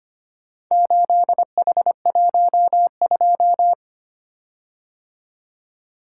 Morse code, heard as a keyed tone, about 25 wpm, 700 Hz. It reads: 8H12
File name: practice / 25wpm / Four Letter-Number Combinations - 25wpm